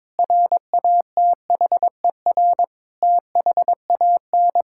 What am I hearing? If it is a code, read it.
Morse code, heard as a keyed tone, about 22 wpm, 700 Hz. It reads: RATHER THAN
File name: practice / 22wpm / Sentences with 2-Word Phrases - 22wpm